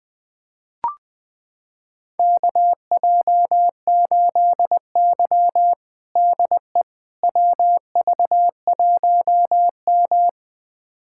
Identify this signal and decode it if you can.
Morse code, heard as a keyed tone, about 21 wpm, 700 Hz. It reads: KJ8Y DE WV1M
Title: USA Callsign DE Callsign - 4 Characters - 20wpm